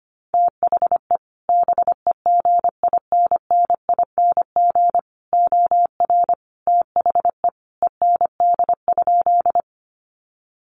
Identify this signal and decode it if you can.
Morse code, heard as a keyed tone, about 25 wpm, 700 Hz. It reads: THE BEGINNING OR THE END?
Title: Sentences from Top 800 Words 25wpm